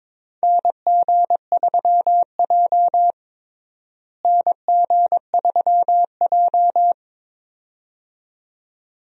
Morse code, heard as a keyed tone, about 22 wpm, 700 Hz. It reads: NG3J NG3J
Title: USA Call Signs - 4 Characters - Repeated 2x - 22wpm